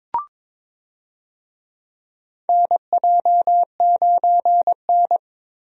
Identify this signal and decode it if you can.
Morse code, heard as a keyed tone, about 22 wpm, 700 Hz. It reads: NJ9N
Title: USA Call Signs - 4 Characters - Repeated 2x - 22wpm